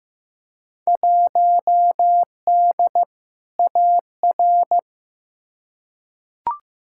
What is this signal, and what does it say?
Morse code, heard as a keyed tone, about 15 wpm, 700 Hz. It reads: 1D AR
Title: Field Day Exchanges - 15wpm